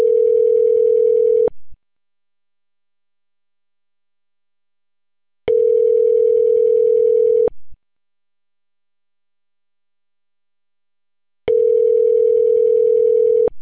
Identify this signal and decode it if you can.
Ringback tone (North America), 440 + 480 Hz.